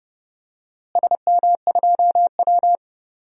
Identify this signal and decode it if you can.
Morse code, heard as a keyed tone, about 30 wpm, 700 Hz. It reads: SM2W
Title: DXCC-100 Call Signs - 4 Characters - Repeated 3x - 30wpm